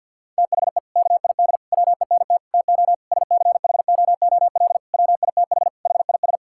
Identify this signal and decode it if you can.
Morse code, heard as a keyed tone, about 50 wpm, 700 Hz. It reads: THE KID WENT TO SCHOOL WITH HIS
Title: Sentences from Top 1600 Words 50wpm